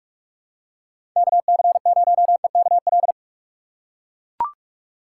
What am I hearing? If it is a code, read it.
Morse code, heard as a keyed tone, about 43 wpm, 700 Hz. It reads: KC0EKL
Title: USA Call Signs - 4x Character Spacing - 45wpm